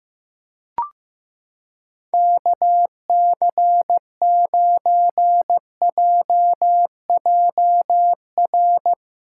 Morse code, heard as a keyed tone, about 15 wpm, 700 Hz. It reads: KC9JJR
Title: USA Callsign DE Callsign - 6 Characters - 15wpm